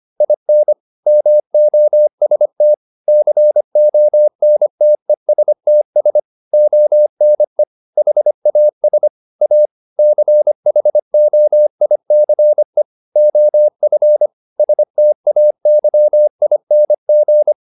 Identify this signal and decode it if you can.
Morse code, heard as a keyed tone, about 25 wpm, 600 Hz. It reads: IN MOST CONTESTS ONE HAS A CHOICE OF STAYING